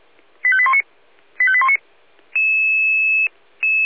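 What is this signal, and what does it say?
Five-tone selective call (ZVEI), heard as quick runs of five tones, tones of about 70 ms. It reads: 85819 85819
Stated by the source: ZVEI five-tone selective call received via SDR, from the Feuerwehr Orlamünde audio server.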